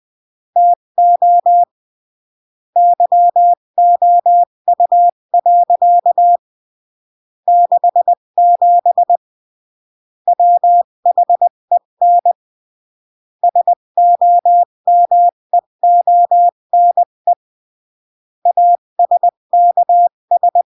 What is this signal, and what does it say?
Morse code, heard as a keyed tone, about 20 wpm, 700 Hz. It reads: TO YOU. 67 WHEN SOMEONE ASKS
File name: CW / jodorowsky / 82 commandments 20-15